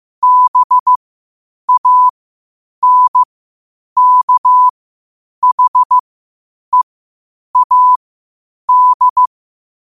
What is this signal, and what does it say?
Morse code, heard as a keyed tone, about 15 wpm, spaced as for 8 wpm, 1000 Hz. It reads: BANKHEAD
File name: qotdfiles / 8wpm